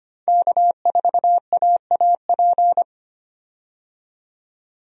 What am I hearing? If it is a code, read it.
Morse code, heard as a keyed tone, about 25 wpm, 700 Hz. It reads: K4AAP
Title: USA Call Signs - 5 Characters - 25wpm